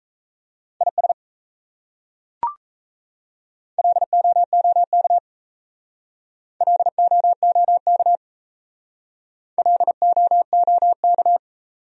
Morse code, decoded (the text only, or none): IS LOOK LOOK LOOK